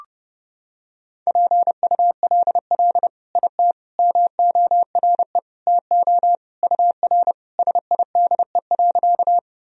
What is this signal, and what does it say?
Morse code, heard as a keyed tone, about 30 wpm, 700 Hz. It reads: PULL IT MORE TO UR SIDE.
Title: Sentences from Top 400 Words and Abbreviations 30wpm